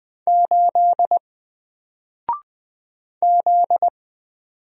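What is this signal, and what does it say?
Morse code, heard as a keyed tone, about 20 wpm, 700 Hz. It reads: 8 Z
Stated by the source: Single letter and number Morse drill at 20 wpm, from N4LCD.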